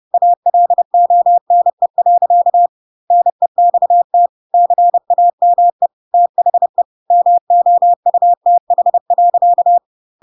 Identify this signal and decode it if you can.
Morse code, heard as a keyed tone, about 30 wpm, 700 Hz. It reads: ALONE. NEXT CAME THE MOUTH.
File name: Pinocchio / Pinocchio ch3 30 30 ik1hge